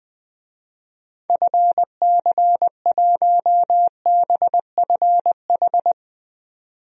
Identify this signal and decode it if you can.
Morse code, heard as a keyed tone, about 20 wpm, 700 Hz. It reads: FC1BFH
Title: DXCC-100 Call Signs - 6 Characters - 20wpm